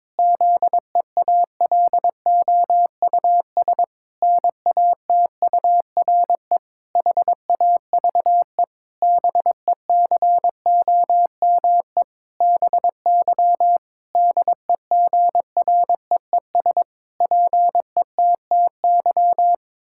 Morse code, heard as a keyed tone, about 22 wpm, 700 Hz. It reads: ZEALOUS NATURE HAVE BECOME BY DEGREES PETTY
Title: Pangram Challenge v2 - 22wpm